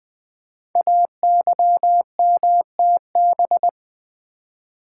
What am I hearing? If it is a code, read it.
Morse code, heard as a keyed tone, about 20 wpm, 700 Hz. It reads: AYMTB
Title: Code Groups - 5 Letters - 20wpm